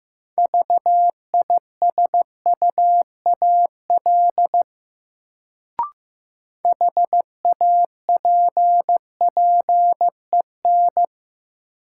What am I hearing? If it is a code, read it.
Morse code, heard as a keyed tone, about 15 wpm, 700 Hz. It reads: VISUAL HAPPEN
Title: Sets of 5 Words Encore - New Word Drill - 15wpm